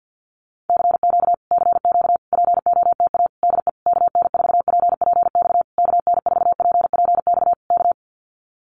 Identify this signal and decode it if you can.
Morse code, heard as a keyed tone, about 50 wpm, 700 Hz. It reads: CQ CQ POTA DE KN4PPX KN4PPX K
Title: Common QSO Phrases Encore 50wpm